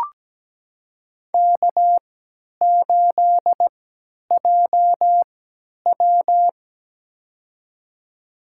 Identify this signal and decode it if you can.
Morse code, heard as a keyed tone, about 17 wpm, spaced as for 10 wpm, 700 Hz. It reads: K8JW
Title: USA Call Signs - 4 Characters - 3x Character Spacing - 17wpm